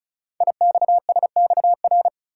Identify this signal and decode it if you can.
Morse code, heard as a keyed tone, about 36 wpm, 700 Hz. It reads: IXSXR
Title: Code Groups - 5 Letter:Numbers - 35wpm